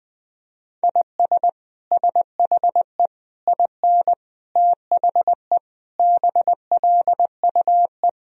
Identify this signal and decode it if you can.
Morse code, heard as a keyed tone, about 21 wpm, 700 Hz. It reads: IS SHE IN THE BLUE
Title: Sentences from Top 1000 Words - Review Part 3 - 20wpm